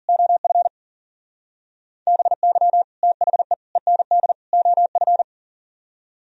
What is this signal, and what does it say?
Morse code, heard as a keyed tone, about 52 wpm, 700 Hz. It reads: OF BY THE END OF